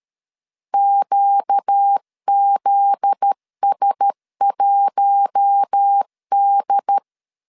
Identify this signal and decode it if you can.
Morse code, heard as a keyed tone, about 13 wpm, 800 Hz. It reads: QZS1D
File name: LEZIONI / 7 GF9Z / 7 GF9Z 12wpm 2147